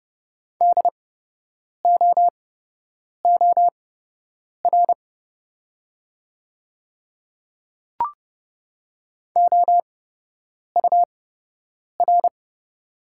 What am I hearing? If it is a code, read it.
Morse code, heard as a keyed tone, about 30 wpm, spaced as for 8 wpm, 690 Hz. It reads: DOOR OUR
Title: 8x Character Spacing - Top 500 Words - 30wpm